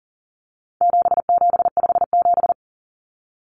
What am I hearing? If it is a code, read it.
Morse code, heard as a keyed tone, about 40 wpm, 700 Hz. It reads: 7757
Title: Numbers - 4-Digit 40wpm